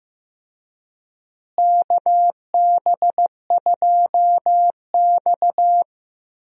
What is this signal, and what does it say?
Morse code, heard as a keyed tone, about 15 wpm, 700 Hz. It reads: KB2X
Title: USA Call Signs - 4 Characters - Repeated 3x - 15wpm